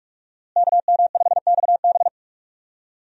Morse code, heard as a keyed tone, about 46 wpm, 700 Hz. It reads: KM5XB
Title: USA Call Signs - 5 Characters - Repeated 3x - 45wpm